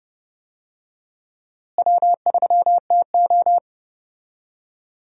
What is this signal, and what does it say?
Morse code, heard as a keyed tone, about 30 wpm, 700 Hz. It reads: W3TO
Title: USA Call Signs - 4 Characters - Repeated 3x - 30wpm